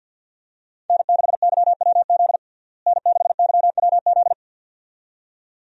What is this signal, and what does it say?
Morse code, heard as a keyed tone, about 49 wpm, 700 Hz. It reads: N6YWZ N6YWZ
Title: USA Call Signs - 5 Characters - Repeated 2x - 50wpm